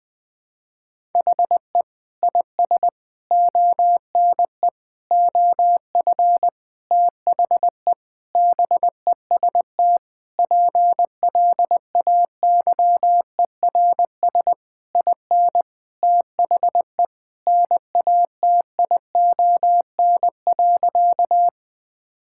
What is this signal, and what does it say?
Morse code, heard as a keyed tone, about 20 wpm, 700 Hz. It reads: HE IS ONE OF THE BEST PLAYERS IN THE NATION.
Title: Sentences with 7-Word Phrases - 20wpm